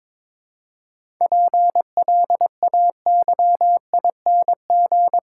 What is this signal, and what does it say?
Morse code, heard as a keyed tone, about 22 wpm, 700 Hz. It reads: PLAYING
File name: Difficult Sentences with Hints - 22wpm